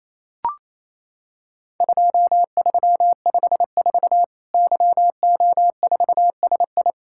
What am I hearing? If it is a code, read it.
Morse code, heard as a keyed tone, about 28 wpm, 700 Hz. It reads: 2354 YO4SI
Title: Worked All Europe DX Contest - QTC Exchange - 28wpm